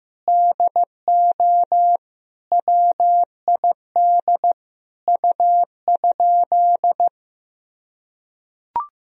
Morse code, heard as a keyed tone, about 15 wpm, 700 Hz. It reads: DO WID U?
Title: Sentences from Top 100 Words and Abbreviations 15wpm